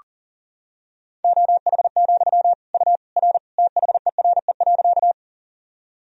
Morse code, heard as a keyed tone, about 40 wpm, 700 Hz. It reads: OH, U R THERE.
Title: Sentences from Top 500 Words and Abbreviations 40wpm